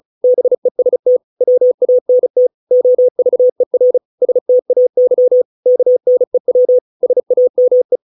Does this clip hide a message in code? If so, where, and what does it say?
Yes: Morse code from 0.2 to 8.0 s, heard as a keyed tone, about 34 wpm, 500 Hz. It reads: BEST WANT OVER STAY KNEW SAME